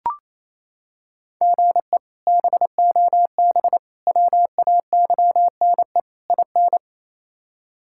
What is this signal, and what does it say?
Morse code, heard as a keyed tone, about 28 wpm, 700 Hz. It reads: GE BOB WAYNE IN
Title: K1USN SST QSO Practice - 28wpm